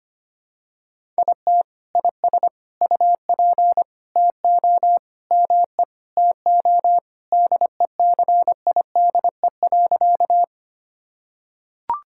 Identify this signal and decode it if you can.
Morse code, heard as a keyed tone, about 25 wpm, 700 Hz. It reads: IT IS UP TO ME TO DECIDE.
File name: practice / 25wpm / Sentences with 2-Word Phrases - 25wpm